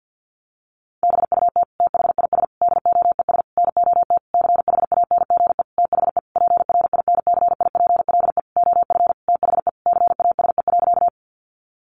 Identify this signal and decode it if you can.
Morse code, heard as a keyed tone, about 50 wpm, 700 Hz. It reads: BUT THIS DOES NOT CHANGE THE PRINCIPLE OF THE CASE.
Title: Sentences from Top 1400 Words 50wpm